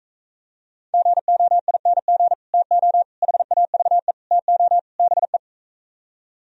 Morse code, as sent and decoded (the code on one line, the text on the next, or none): --. --- .. -. --. / - --- / .... .- ...- . / - --- / -... .
GOING TO HAVE TO BE